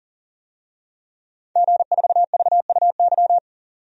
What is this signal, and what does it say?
Morse code, heard as a keyed tone, about 40 wpm, 700 Hz. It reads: G4VUY
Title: DXCC-100 Call Signs - 5 Characters - 40wpm